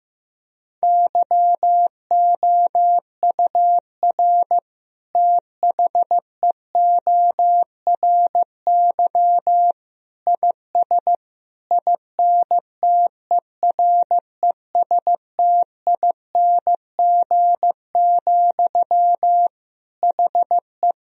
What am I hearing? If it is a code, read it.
Morse code, heard as a keyed tone, about 16 wpm, 700 Hz. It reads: YOUR THEORY IS INTERESTING, HE
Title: Sentences from Top 1000 Words - Review Part 3 - 15wpm